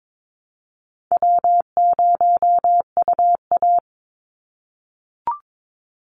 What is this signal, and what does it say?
Morse code, heard as a keyed tone, about 22 wpm, 700 Hz. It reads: W0UA
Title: Easy Call Signs - 22wpm